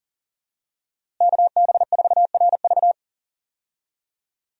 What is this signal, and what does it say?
Morse code, heard as a keyed tone, about 40 wpm, 700 Hz. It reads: KB4RV
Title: USA Call Signs - 5 Characters - Repeated 3x - 40wpm